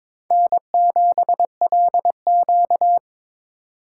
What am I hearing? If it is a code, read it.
Morse code, heard as a keyed tone, about 21 wpm, 700 Hz. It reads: N7LQ
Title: USA Call Signs - 4 Characters - 3x Character Spacing - 22wpm